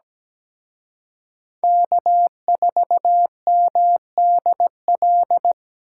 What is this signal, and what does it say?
Morse code, heard as a keyed tone, about 20 wpm, 700 Hz. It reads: K4MDL
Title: Speed Racing - USA 1x3 Call Signs 17wpm